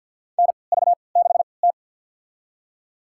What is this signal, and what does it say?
Morse code, heard as a keyed tone, about 48 wpm, spaced as for 28 wpm, 700 Hz. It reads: NV6T
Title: USA Call Signs - 4 Characters - 3x Character Spacing - 50wpm